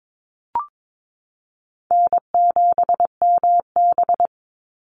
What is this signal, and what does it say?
Morse code, heard as a keyed tone, about 22 wpm, 700 Hz. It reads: N7MB
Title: Easy Call Signs Repeated 2x - 22wpm